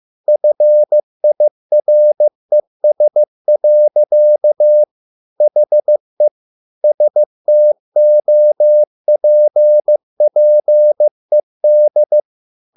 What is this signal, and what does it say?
Morse code, heard as a keyed tone, about 15 wpm, 600 Hz. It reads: FIRES. HE STOPPED